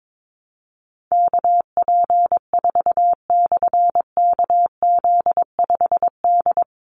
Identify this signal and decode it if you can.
Morse code, heard as a keyed tone, about 22 wpm, 700 Hz. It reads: KP4/KZ5D
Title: Call Signs - 22wpm